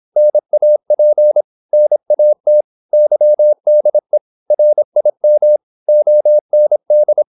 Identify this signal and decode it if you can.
Morse code, heard as a keyed tone, about 26 wpm, 600 Hz. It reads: NAP NAT YDE RIM OND
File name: Korte ord DK 26wpm